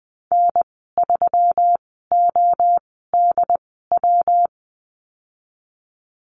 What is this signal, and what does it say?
Morse code, heard as a keyed tone, about 20 wpm, spaced as for 14 wpm, 700 Hz. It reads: N3ODW
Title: 179 Lesson 29g - Slash - Call Signs 20wpm